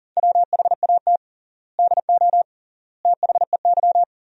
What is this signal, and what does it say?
Morse code, heard as a keyed tone, about 40 wpm, 700 Hz. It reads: WHAT DO THEY
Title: 3x Word-Spacing - Sentences from Top 200 Words - 40wpm